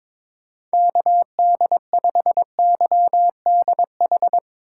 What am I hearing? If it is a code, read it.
Morse code, heard as a keyed tone, about 22 wpm, 700 Hz. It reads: KD5YDH
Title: USA Call Signs - 6 Characters - 2x Character Spacing - 22wpm